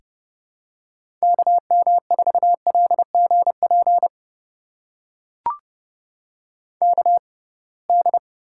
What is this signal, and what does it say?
Morse code, heard as a keyed tone, about 30 wpm, spaced as for 23 wpm, 700 Hz. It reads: KM4LGP KD
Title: USA Call Signs - 6 Characters - 6x Character Spacing - 30wpm